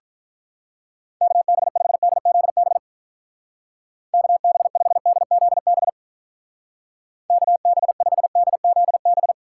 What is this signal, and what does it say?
Morse code, heard as a keyed tone, about 43 wpm, 700 Hz. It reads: KB5DZB KB5DZB KB5DZB
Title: Speed Racing - USA 2x3 Call Signs 35wpm